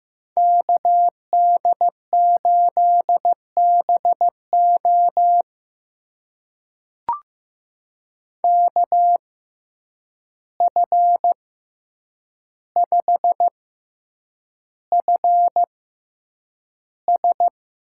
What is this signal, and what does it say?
Morse code, heard as a keyed tone, about 15 wpm, spaced as for 8 wpm, 700 Hz. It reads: KD8BO KF5FS
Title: USA Call Signs - 5 Characters - 6x Character Spacing - 15wpm